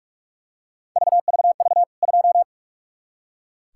Morse code, heard as a keyed tone, about 46 wpm, 700 Hz. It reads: VVV 2
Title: K1USN SST QSO Practice 45wpm